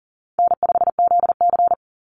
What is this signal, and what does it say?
Morse code, heard as a keyed tone, about 39 wpm, 700 Hz. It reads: N5ZC
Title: Easy Call Signs - 40wpm